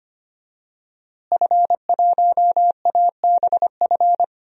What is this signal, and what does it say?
Morse code, heard as a keyed tone, about 25 wpm, 700 Hz. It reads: F1ABF
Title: DXCC-100 Call Signs - 5 Characters - Repeated 3x - 25wpm